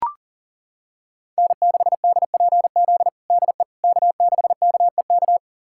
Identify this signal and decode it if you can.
Morse code, heard as a keyed tone, about 37 wpm, 700 Hz. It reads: N6DPZ DE K6KEK
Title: USA Callsign DE Callsign - 5 Characters - 40wpm